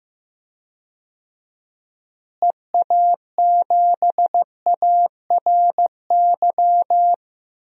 Morse code, heard as a keyed tone, about 15 wpm, 700 Hz. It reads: EA7ARY
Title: DXCC-100 Call Signs - 6 Characters - Repeated 3x - 15wpm